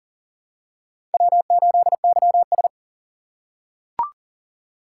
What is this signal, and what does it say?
Morse code, heard as a keyed tone, about 40 wpm, 700 Hz. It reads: W8YS
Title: USA Call Signs - 4 Characters - 2x Character Spacing - 40wpm